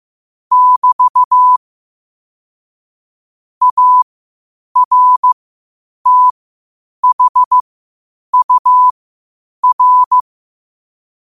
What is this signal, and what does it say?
Morse code, heard as a keyed tone, about 15 wpm, spaced as for 8 wpm, 1000 Hz. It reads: <BT> ARTHUR
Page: New quotes every day in morse code at 8 Words per minute.